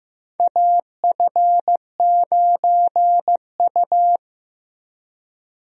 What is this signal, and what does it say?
Morse code, heard as a keyed tone, about 15 wpm, 700 Hz. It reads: AF9U
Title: USA Call Signs - 4 Characters - Repeated 2x - 15wpm